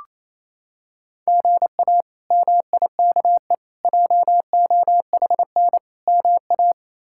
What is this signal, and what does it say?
Morse code, heard as a keyed tone, about 28 wpm, 700 Hz. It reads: GA MIKE JOHN MA
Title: K1USN SST QSO Practice - 28wpm